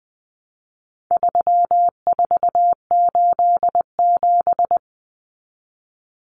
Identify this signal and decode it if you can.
Morse code, heard as a keyed tone, about 20 wpm, 700 Hz. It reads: 3487
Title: Numbers - 4-Digit 20wpm